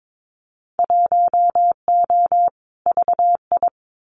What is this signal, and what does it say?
Morse code, heard as a keyed tone, about 22 wpm, 700 Hz. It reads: 1O VI
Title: Winter Field Day Exchanges Encore 22wpm